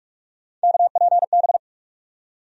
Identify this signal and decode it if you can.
Morse code, heard as a keyed tone, about 46 wpm, 700 Hz. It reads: KPB
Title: USA Call Sign Postfixes - 3 Letters 45wpm